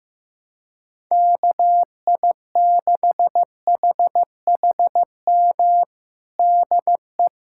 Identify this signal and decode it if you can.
Morse code, heard as a keyed tone, about 15 wpm, 700 Hz. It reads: KI6HHM DE
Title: USA Callsign DE Callsign 15wpm